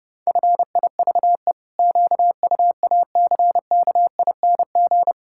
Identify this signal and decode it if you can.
Morse code, heard as a keyed tone, about 30 wpm, 700 Hz. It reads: FIVE QUACKING
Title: Pangram Challenge v2 - 30wpm